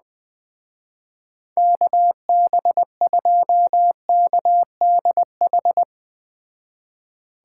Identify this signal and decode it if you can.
Morse code, heard as a keyed tone, about 20 wpm, 700 Hz. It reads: KB2KDH